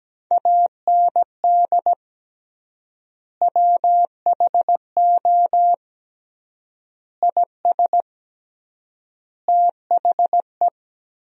Morse code, heard as a keyed tone, about 17 wpm, 700 Hz. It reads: AND WHO IS THE
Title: 3x Word-Spacing - Sentences from Top 200 Words - 17wpm